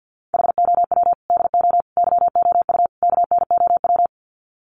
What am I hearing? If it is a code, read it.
Morse code, heard as a keyed tone, about 50 wpm, 700 Hz. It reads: HOW DO YOU KNOW
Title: Sets of 4 Words 50wpm